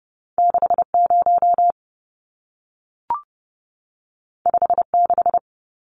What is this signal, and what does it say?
Morse code, heard as a keyed tone, about 30 wpm, 700 Hz. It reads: 60 56
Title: Numbers - 2-Digit 30wpm